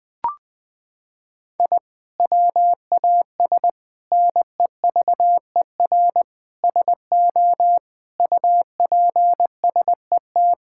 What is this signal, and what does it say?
Morse code, heard as a keyed tone, about 20 wpm, 700 Hz. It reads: I WAS NEVER SO UPSET
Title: Sentences from Top 1000 Words - Review Part 5 - 20wpm